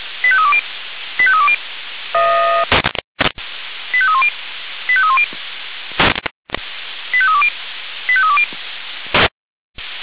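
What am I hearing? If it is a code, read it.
Five-tone selective call (ZVEI), heard as quick runs of five tones, tones of about 70 ms. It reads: 85320 85320 85310 85310 85320 85320
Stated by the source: ZVEI five-tone call received by SDR, archived on the Feuerwehr Orlamünde audio server (f85360).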